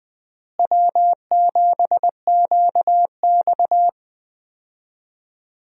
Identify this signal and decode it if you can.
Morse code, heard as a keyed tone, about 20 wpm, 700 Hz. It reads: W7QX
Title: USA Call Signs - 4 Characters - Repeated 2x - 20wpm